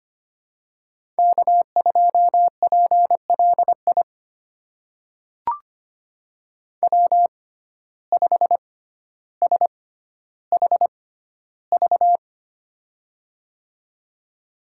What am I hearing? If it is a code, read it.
Morse code, heard as a keyed tone, about 25 wpm, spaced as for 13 wpm, 700 Hz. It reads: K2PLI W5SHV
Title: USA Call Signs - 5 Characters - 6x Character Spacing - 25wpm